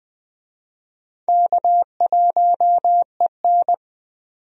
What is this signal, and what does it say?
Morse code, heard as a keyed tone, about 20 wpm, 700 Hz. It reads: K1EN
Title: USA Call Signs - 4 Characters - 20wpm